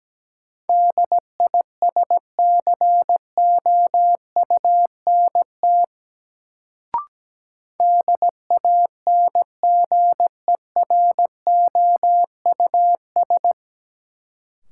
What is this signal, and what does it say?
Morse code, heard as a keyed tone, about 17 wpm, 700 Hz. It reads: DISCOUNT DANGEROUS
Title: Sentences from Top 1200 Words - New Word Drill Alt - 17wpm